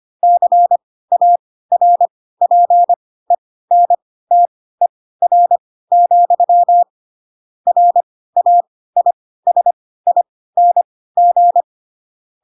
Morse code, heard as a keyed tone, about 25 wpm, spaced as for 16 wpm, 700 Hz. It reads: CARPENTER, RAISING
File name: Pinocchio / Pinocchio ch2 10 25 ik1hge